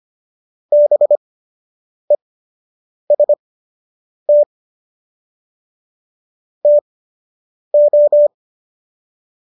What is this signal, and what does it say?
Morse code, heard as a keyed tone, about 25 wpm, spaced as for 8 wpm, 590 Hz. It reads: BEST TO